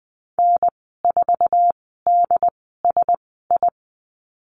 Morse code, heard as a keyed tone, about 20 wpm, spaced as for 14 wpm, 700 Hz. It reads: N4DSI
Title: 069 Lesson 12g - Letter d - Call Signs 20wpm